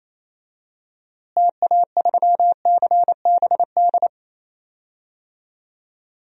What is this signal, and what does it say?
Morse code, heard as a keyed tone, about 28 wpm, 700 Hz. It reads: TA3CBD
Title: DXCC-100 Call Signs - 6 Characters - Repeated 3x - 28wpm